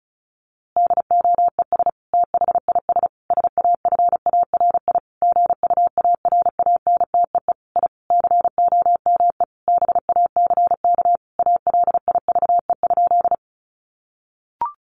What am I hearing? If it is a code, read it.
Morse code, heard as a keyed tone, about 34 wpm, 700 Hz. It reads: DOES THIS SAFARI GUARANTEE I COME BACK ALIVE?
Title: A Sound Of Thunder 35wpm